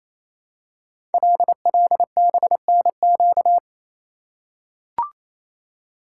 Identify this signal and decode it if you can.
Morse code, heard as a keyed tone, about 27 wpm, 700 Hz. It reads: LLBNQ
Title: Code Groups - 5 Letters - 28wpm